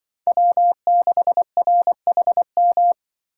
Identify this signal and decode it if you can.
Morse code, heard as a keyed tone, about 25 wpm, 700 Hz. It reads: W6RHM